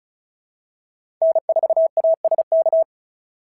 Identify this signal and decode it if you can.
Morse code, heard as a keyed tone, about 36 wpm, 690 Hz. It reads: N4ASK
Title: SOTA QSO Practice 35wpm